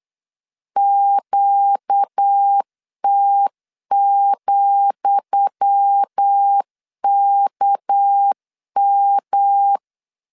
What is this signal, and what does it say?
Morse code, heard as a keyed tone, about 8 wpm, 800 Hz. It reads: QT,KM
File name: LEZIONI / 10 SIGNS / 10D COMA 8wpm 3749